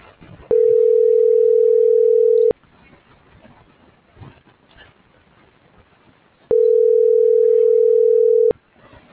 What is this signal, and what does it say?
Ringback tone (North America), 440 + 480 Hz.
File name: CENTRAL GENERAL / out tania.lemus - .1721